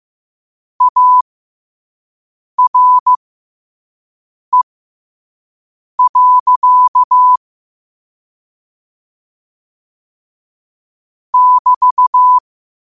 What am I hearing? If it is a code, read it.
Morse code, heard as a keyed tone, about 15 wpm, spaced as for 5 wpm, 1000 Hz. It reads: ARE. <BT>
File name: 5wpm